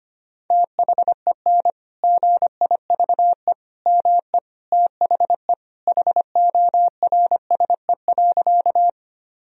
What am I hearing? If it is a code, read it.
Morse code, heard as a keyed tone, about 25 wpm, 700 Hz. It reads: THEN GIVE ME THE HORSE.
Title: Sentences from Top 1000 Words - Review Part 1 - 25wpm